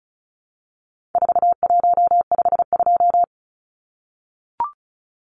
Morse code, heard as a keyed tone, about 34 wpm, 700 Hz. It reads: 4152
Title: Numbers - 4-Digit 35wpm